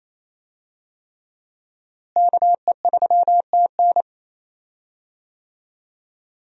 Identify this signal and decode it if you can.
Morse code, heard as a keyed tone, about 28 wpm, 700 Hz. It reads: KE3TN
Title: USA Call Signs - 5 Characters - Repeated 3x - 28wpm